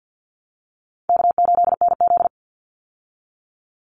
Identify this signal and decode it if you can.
Morse code, heard as a keyed tone, about 50 wpm, 700 Hz. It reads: K8NZ
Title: Easy Call Signs Repeated 3x - 50wpm